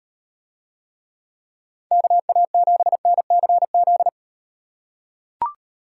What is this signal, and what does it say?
Morse code, heard as a keyed tone, about 38 wpm, 700 Hz. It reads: KA7NCZ